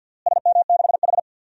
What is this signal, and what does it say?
Morse code, heard as a keyed tone, about 53 wpm, 700 Hz. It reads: SM6H
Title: DXCC-100 Call Signs - 4 Characters - Repeated 2x - 50wpm